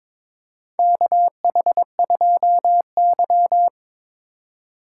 Morse code, heard as a keyed tone, about 22 wpm, 700 Hz. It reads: KH2Y
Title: USA Call Signs - 3x Character Spacing - 22wpm